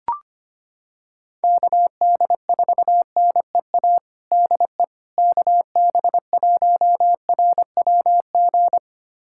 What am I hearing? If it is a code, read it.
Morse code, heard as a keyed tone, about 25 wpm, 700 Hz. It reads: KD4NEA DE KB1RWG
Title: USA Callsign DE Callsign - 6 Characters - 25wpm